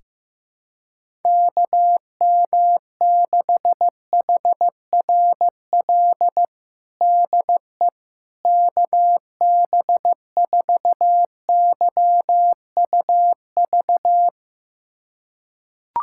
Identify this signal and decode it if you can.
Morse code, heard as a keyed tone, about 15 wpm, 700 Hz. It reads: KM6HRL DE KB4YUV
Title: USA Callsign DE Callsign 15wpm